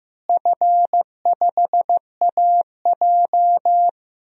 Morse code, heard as a keyed tone, about 15 wpm, 700 Hz. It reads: F5AJ
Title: DXCC-100 Call Signs - 4 Characters - Repeated 3x - 15wpm